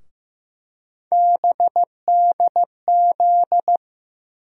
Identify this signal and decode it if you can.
Morse code, heard as a keyed tone, about 15 wpm, 700 Hz. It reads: BDZ
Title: USA Call Sign Postfixes - 3 Letters 15wpm